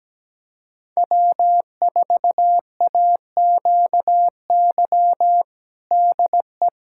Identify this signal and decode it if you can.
Morse code, heard as a keyed tone, about 17 wpm, 700 Hz. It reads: W4AQY DE
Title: USA Callsign DE Callsign - 5 Characters - 17wpm